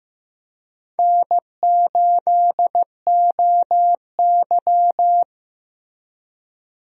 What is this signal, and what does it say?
Morse code, heard as a keyed tone, about 15 wpm, 700 Hz. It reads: N8OY
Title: USA Call Signs - 4 Characters - Repeated 2x - 15wpm